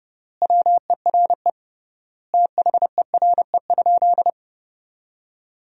Morse code, heard as a keyed tone, about 30 wpm, 700 Hz. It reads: WERE THERE?
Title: 3x Word-Spacing - Sentences from Top 500 Words - 30wpm